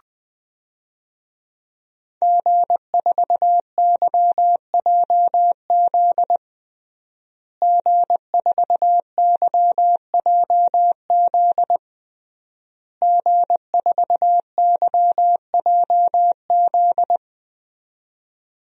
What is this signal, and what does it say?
Morse code, heard as a keyed tone, about 20 wpm, 700 Hz. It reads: G4YJZ G4YJZ G4YJZ
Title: DXCC-100 Call Signs - 5 Characters - Repeated 3x - 20wpm